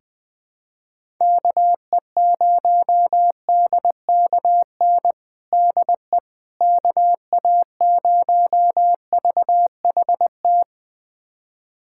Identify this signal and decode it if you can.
Morse code, heard as a keyed tone, about 20 wpm, 700 Hz. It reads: KE0DKN DE KA0VHT
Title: USA Callsign DE Callsign - 6 Characters - 20wpm